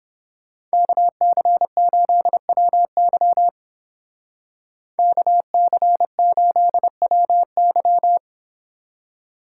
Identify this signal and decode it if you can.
Morse code, heard as a keyed tone, about 30 wpm, 700 Hz. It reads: KC8WY KC8WY